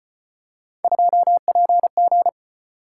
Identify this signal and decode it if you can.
Morse code, heard as a keyed tone, about 35 wpm, 700 Hz. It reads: 2PG